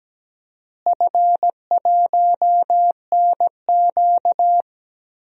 Morse code, heard as a keyed tone, about 17 wpm, 700 Hz. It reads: F1NQ
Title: DXCC-100 Call Signs - 4 Characters - Repeated 2x - 17wpm